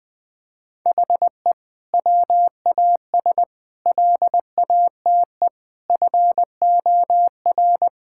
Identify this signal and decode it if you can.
Morse code, heard as a keyed tone, about 20 wpm, 700 Hz. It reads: HE WAS LATE FOR
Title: Sentences from Top 1600 Words 20wpm